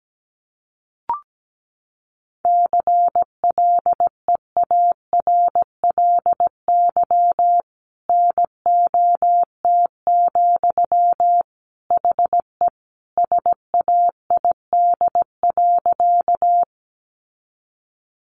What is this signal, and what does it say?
Morse code, heard as a keyed tone, about 17 wpm, 700 Hz. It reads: CLEARLY NOT, HE SAID.
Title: Sentences from Top 1300 Words 17wpm